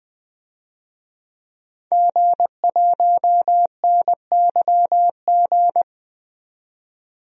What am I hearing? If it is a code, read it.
Morse code, heard as a keyed tone, about 21 wpm, 700 Hz. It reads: G1NYG
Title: DXCC-100 Call Signs - 5 Characters - 20wpm